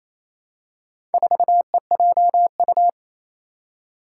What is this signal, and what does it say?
Morse code, heard as a keyed tone, about 28 wpm, 700 Hz. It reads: 4EJU
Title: Four Challenging Letter-Number Combinations 28wpm